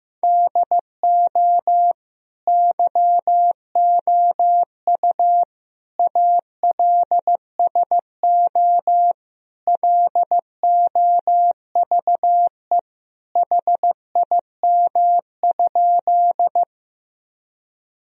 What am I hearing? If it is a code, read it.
Morse code, heard as a keyed tone, about 15 wpm, 700 Hz. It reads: DO YOU ALSO LOVE HIM?
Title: Sentences from Top 1000 Words - Review Part 1 - 15wpm